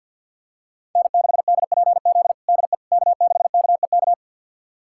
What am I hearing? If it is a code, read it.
Morse code, heard as a keyed tone, about 51 wpm, 700 Hz. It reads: N6DPZ DE K6KEK
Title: USA Callsign DE Callsign - 5 Characters - 50wpm